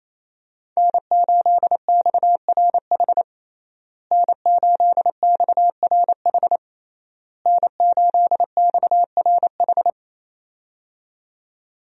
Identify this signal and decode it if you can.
Morse code, heard as a keyed tone, about 28 wpm, 700 Hz. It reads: N8XRH N8XRH N8XRH
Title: USA Call Signs - 5 Characters - Repeated 3x - 28wpm